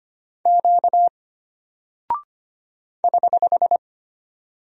Morse code, heard as a keyed tone, about 25 wpm, 700 Hz. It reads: Q <HH>
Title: Single Character (Letter:Number:Punctuation:Prosign) Alt 25wpm